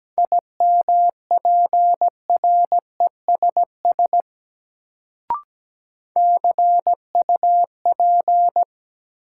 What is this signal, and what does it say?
Morse code, heard as a keyed tone, about 17 wpm, 700 Hz. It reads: IMPRESS CUP
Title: Sentences from Top 1600 Words - New Word Drill - 17wpm